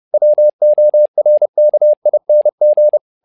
Morse code, heard as a keyed tone, about 30 wpm, 600 Hz. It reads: WORKING
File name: learning center / tower trouble / tower trouble 30wpm